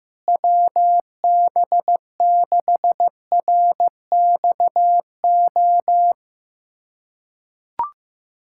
Morse code, heard as a keyed tone, about 15 wpm, 700 Hz. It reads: WB6RXO
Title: USA Callsign DE Callsign - 6 Characters - 15wpm